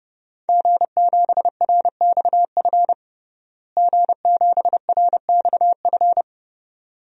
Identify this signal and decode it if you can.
Morse code, heard as a keyed tone, about 32 wpm, 700 Hz. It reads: G7RXF G7RXF
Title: DXCC-100 Call Signs - 5 Characters - Repeated 2x - 30wpm